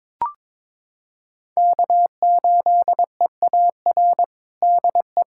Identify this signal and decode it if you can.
Morse code, heard as a keyed tone, about 22 wpm, 700 Hz. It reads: K8EAR DE
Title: USA Callsign DE Callsign - 5 Characters - 22wpm